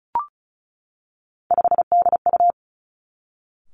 Morse code, heard as a keyed tone, about 36 wpm, 700 Hz. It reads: 5DU
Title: Three Challenging Letter-Number Combinations 35wpm